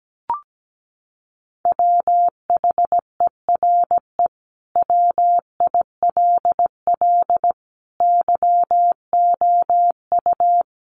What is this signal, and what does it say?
Morse code, heard as a keyed tone, about 17 wpm, 700 Hz. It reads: WHERE WILL YOU
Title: Sentences from Top 1300 Words 17wpm